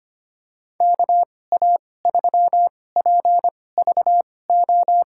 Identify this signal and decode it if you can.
Morse code, heard as a keyed tone, about 25 wpm, spaced as for 18 wpm, 700 Hz. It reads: KA3PVO
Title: USA Call Signs - 6 Characters - 2x Character Spacing - 25wpm